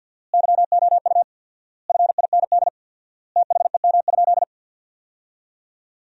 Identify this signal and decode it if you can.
Morse code, heard as a keyed tone, about 50 wpm, 700 Hz. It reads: YOU FIND THEM?
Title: 4x Word-Spacing - Sentences from Top 200 Words - 50wpm